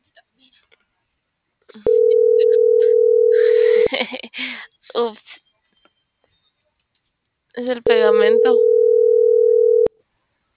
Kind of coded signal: telephone call progress tone

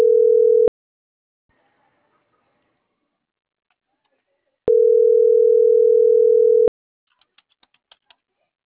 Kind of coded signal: telephone call progress tone

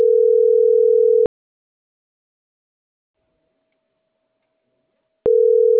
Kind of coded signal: telephone call progress tone

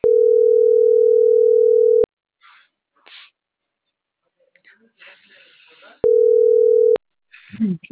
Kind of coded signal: telephone call progress tone

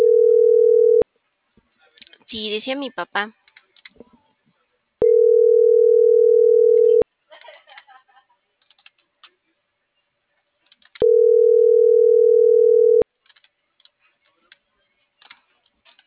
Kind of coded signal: telephone call progress tone

